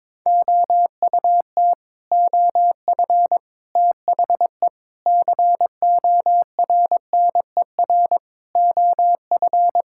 Morse code, heard as a keyed tone, about 22 wpm, 700 Hz. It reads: OUT OF THE CORNER OF
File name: practice / 22wpm / Sets of 8 Words - 22wpm